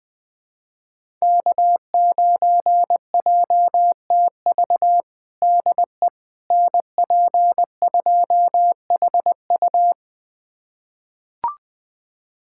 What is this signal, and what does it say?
Morse code, heard as a keyed tone, about 20 wpm, 700 Hz. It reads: K9JTV DE NP2HU
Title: USA Callsign DE Callsign - 5 Characters - 20wpm